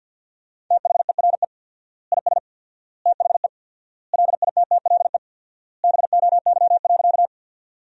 Morse code, heard as a keyed tone, about 53 wpm, 700 Hz. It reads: THERE IS THE LITTLE BOY.
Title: 4x Word-Spacing - Sentences from Top 200 Words - 50wpm